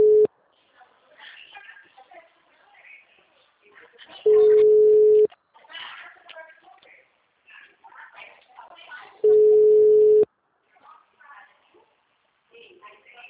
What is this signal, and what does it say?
Ringback tone (Europe, 425 Hz), with speech.